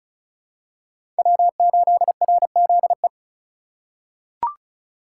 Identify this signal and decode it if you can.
Morse code, heard as a keyed tone, about 34 wpm, 700 Hz. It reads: W8RZE
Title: USA Call Signs - 5 Characters - 2x Character Spacing - 35wpm